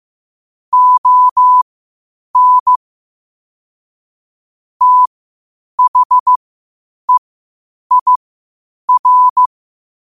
Morse code, heard as a keyed tone, about 15 wpm, spaced as for 8 wpm, 1000 Hz. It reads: ON THEIR
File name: qotdfiles / 8wpm